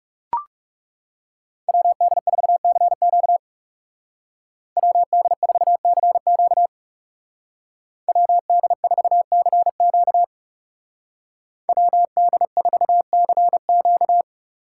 Morse code, heard as a keyed tone, about 37 wpm, 700 Hz. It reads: WD4CQ WD4CQ WD4CQ WD4CQ
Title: Speed Racing - USA 2x2 Call Signs 30wpm